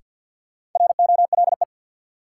Morse code, heard as a keyed tone, about 53 wpm, 700 Hz. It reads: ROLE
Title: Sets of 3 Words - New Word Drill - 50wpm